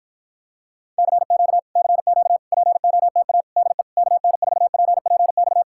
Morse code, heard as a keyed tone, about 52 wpm, 700 Hz. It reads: CQ CQ POTA DE KN4PPX